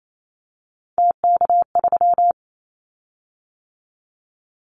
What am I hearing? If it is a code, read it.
Morse code, heard as a keyed tone, about 28 wpm, 700 Hz. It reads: TK3
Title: Callsign Prefixes for Top 100 DXCC - 28wpm